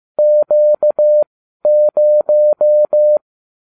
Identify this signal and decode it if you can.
Morse code, heard as a keyed tone, about 15 wpm, spaced as for 12 wpm, 600 Hz. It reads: Q0